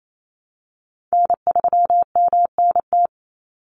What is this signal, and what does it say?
Morse code, heard as a keyed tone, about 28 wpm, 690 Hz. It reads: N3MNT
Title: Call Signs Repeated 3x - 28wpm